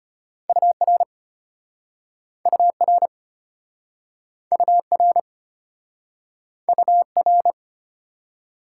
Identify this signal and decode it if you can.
Morse code, heard as a keyed tone, about 33 wpm, 700 Hz. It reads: UR UR UR UR